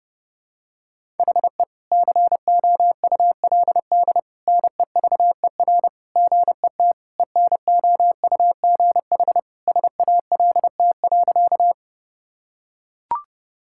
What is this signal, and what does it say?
Morse code, heard as a keyed tone, about 30 wpm, 700 Hz. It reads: HE COULD NEVER GET ENOUGH SALT.
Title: Sentences from Top 1000 Words - Review Part 5 - 30wpm